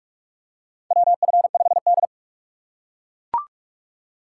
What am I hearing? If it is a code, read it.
Morse code, heard as a keyed tone, about 43 wpm, 700 Hz. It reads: WF5D
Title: USA Call Signs - 4 Characters - 3x Character Spacing - 45wpm